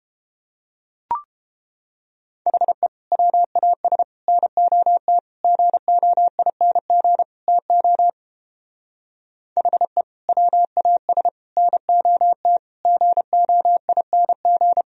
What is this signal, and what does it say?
Morse code, heard as a keyed tone, about 29 wpm, 700 Hz. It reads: HE WAS NOT GOING TO HE WAS NOT GOING